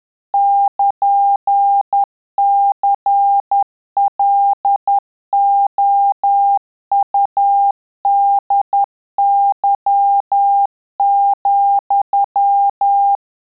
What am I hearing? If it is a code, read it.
Morse code, heard as a keyed tone, about 10 wpm, 800 Hz. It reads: <KN> CLOUDY,
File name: Telegraphy-group / QSO CW 50 CPM